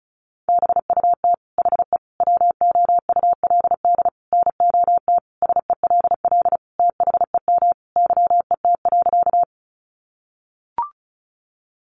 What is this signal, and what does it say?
Morse code, heard as a keyed tone, about 34 wpm, 700 Hz. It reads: BUT HE WOULD NOT SELL THEM YET.
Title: Sentences from Top 1100 Words 35wpm